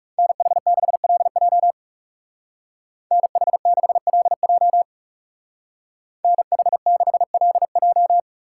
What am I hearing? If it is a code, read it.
Morse code, heard as a keyed tone, about 36 wpm, 700 Hz. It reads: NH6LJ NH6LJ NH6LJ
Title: Speed Racing - USA 2x2 Call Signs 30wpm